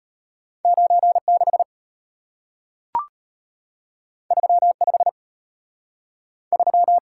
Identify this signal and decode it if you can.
Morse code, heard as a keyed tone, about 34 wpm, 700 Hz. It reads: 96 35 3